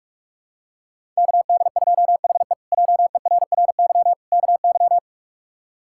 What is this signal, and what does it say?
Morse code, heard as a keyed tone, about 46 wpm, 700 Hz. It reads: KD2HE JERRY KY
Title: K1USN SST QSO Practice - 45wpm